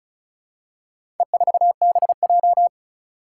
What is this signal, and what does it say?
Morse code, heard as a keyed tone, about 34 wpm, 700 Hz. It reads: E4BJ
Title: Four Challenging Letter-Number Combinations 35wpm